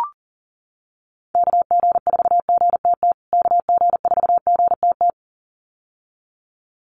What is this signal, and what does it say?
Morse code, heard as a keyed tone, about 40 wpm, 700 Hz. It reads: KG4GTT KG4GTT
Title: USA Call Signs Repeated 2x 40wpm